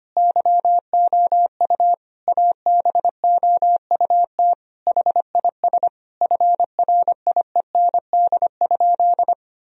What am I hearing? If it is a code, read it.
Morse code, heard as a keyed tone, about 25 wpm, 700 Hz. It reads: YOU ABOUT HIS FRIEND?
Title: Sentences from Top 1000 Words - Review Part 3 - 25wpm